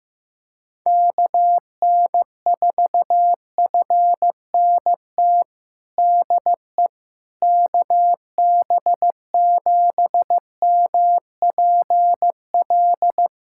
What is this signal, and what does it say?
Morse code, heard as a keyed tone, about 15 wpm, 700 Hz. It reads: KN4FNT DE KB7MPL
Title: USA Callsign DE Callsign - 6 Characters - 15wpm